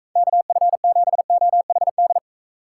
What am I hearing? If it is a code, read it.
Morse code, heard as a keyed tone, about 36 wpm, 700 Hz. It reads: KF7OHD